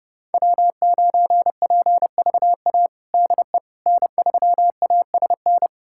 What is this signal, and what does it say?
Morse code, heard as a keyed tone, about 30 wpm, 700 Hz. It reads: W9PVA DE N3ASN
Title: USA Callsign DE Callsign - 5 Characters - 30wpm